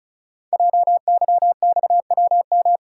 Morse code, heard as a keyed tone, about 36 wpm, 690 Hz. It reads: JYXWM
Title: Code Groups - 5 Letters - 35wpm